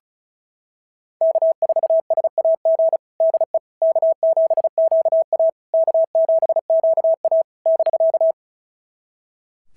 Morse code, heard as a keyed tone, about 33 wpm, 650 Hz. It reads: K4SAG DE K7QA K7QA <BK>
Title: SOTA QSO Practice 35wpm